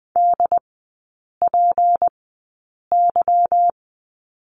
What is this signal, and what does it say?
Morse code, heard as a keyed tone, about 20 wpm, spaced as for 8 wpm, 700 Hz. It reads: DPY
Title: 191 Lesson 31e - Letter v - Three Characters 20wpm